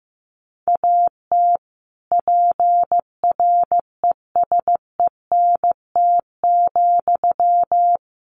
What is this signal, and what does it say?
Morse code, heard as a keyed tone, about 15 wpm, 700 Hz. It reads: AT PRESENT,
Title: Sentences from Top 800 Words 15wpm